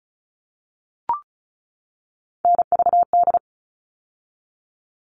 Morse code, heard as a keyed tone, about 34 wpm, 700 Hz. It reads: NVD
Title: All Cut Numbers - 3-Digit 35wpm